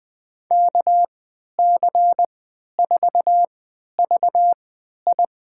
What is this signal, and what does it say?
Morse code, heard as a keyed tone, about 21 wpm, spaced as for 12 wpm, 700 Hz. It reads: KC4VI
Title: USA Call Signs - 3x Character Spacing - 20wpm